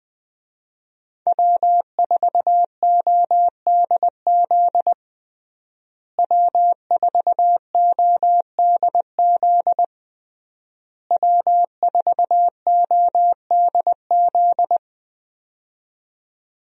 Morse code, heard as a keyed tone, about 20 wpm, 700 Hz. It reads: W4ODZ W4ODZ W4ODZ
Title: USA Call Signs - 5 Characters - Repeated 3x - 20wpm